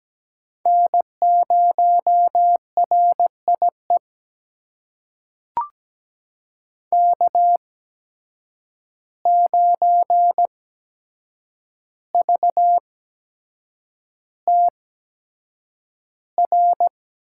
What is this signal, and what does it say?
Morse code, heard as a keyed tone, about 17 wpm, spaced as for 7 wpm, 700 Hz. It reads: N0RIE K9VTR
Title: USA Call Signs - 5 Characters - 8x Character Spacing - 17wpm